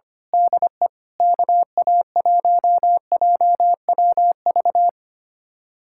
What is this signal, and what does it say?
Morse code, heard as a keyed tone, about 26 wpm, 700 Hz. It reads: DE KA1JWV
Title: Speed-Racing Common QSO Phrases - 25wpm